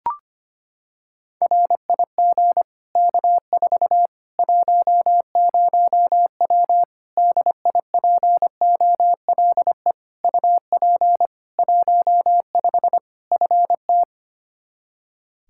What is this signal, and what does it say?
Morse code, heard as a keyed tone, about 25 wpm, 700 Hz. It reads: RIG K4 10W DIPOLE UP 15 FT
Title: Practice QSO - Exchange 2 - Info - 25wpm